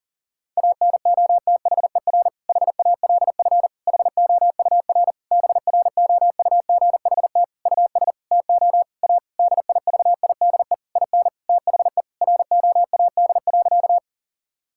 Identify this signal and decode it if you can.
Morse code, heard as a keyed tone, about 37 wpm, 700 Hz. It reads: ANOTHER HALF HOUR BROUGHT US TO A DIVIDE IN THE ROAD.
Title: Sentences from Top 1600 Words 40wpm